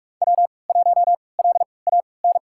Morse code, heard as a keyed tone, about 44 wpm, spaced as for 25 wpm, 700 Hz. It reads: W1LAN
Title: USA Call Signs - 3x Character Spacing - 45wpm